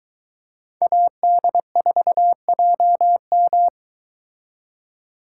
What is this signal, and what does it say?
Morse code, heard as a keyed tone, about 22 wpm, 700 Hz. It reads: AD4JM